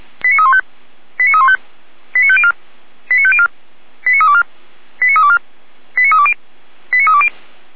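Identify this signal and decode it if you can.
Five-tone selective call (ZVEI), heard as quick runs of five tones, tones of about 70 ms. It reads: 89315 89315 89504 89504 89324 89324 89320 89320
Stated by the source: ZVEI five-tone selective call received with an SDR for Feuerwehr Orlamünde, a fire brigade.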